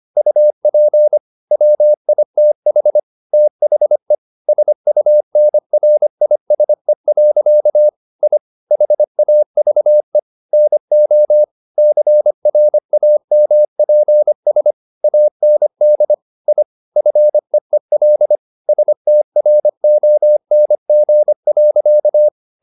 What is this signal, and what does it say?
Morse code, heard as a keyed tone, about 25 wpm, 600 Hz. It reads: UP WITH THE SUNRISE. I HAVE NO CRAMPS AND I FEEL STRONG.
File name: the old man and the sea part 1 25wpm